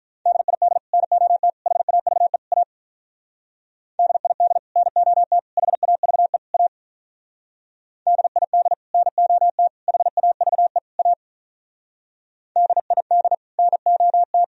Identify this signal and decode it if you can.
Morse code, heard as a keyed tone, about 45 wpm, 700 Hz. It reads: DID NOT HAVE A DID NOT HAVE A DID NOT HAVE A DID NOT